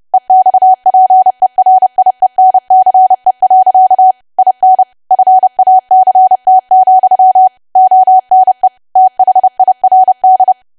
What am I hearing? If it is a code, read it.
Morse code, heard as a keyed tone, about 30 wpm, 750 Hz. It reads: EXPERIENCE. IN FACT, ONE THIRD
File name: cwa / W1AW-Level3 / 30WPM